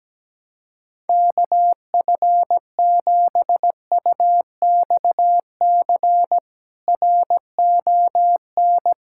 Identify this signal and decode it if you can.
Morse code, heard as a keyed tone, about 17 wpm, 700 Hz. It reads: KF7UXC RON
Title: K1USN SST QSO Practice - 17wpm